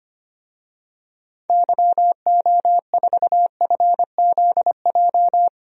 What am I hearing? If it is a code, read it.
Morse code, heard as a keyed tone, about 25 wpm, 700 Hz. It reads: YO4FZJ
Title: DXCC-100 Call Signs - 6 Characters - 25wpm